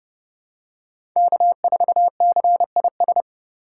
Morse code, heard as a keyed tone, about 32 wpm, 700 Hz. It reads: K4CIS
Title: USA Call Signs - 2x Character Spacing - 30wpm